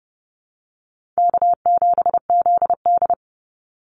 Morse code, heard as a keyed tone, about 30 wpm, 700 Hz. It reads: K7ZD
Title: Easy Call Signs - 30wpm